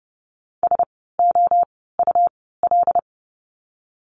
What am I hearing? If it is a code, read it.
Morse code, heard as a keyed tone, about 30 wpm, spaced as for 17 wpm, 700 Hz. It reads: SOUL
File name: Lesson 17f - Letter u - Words 30wpm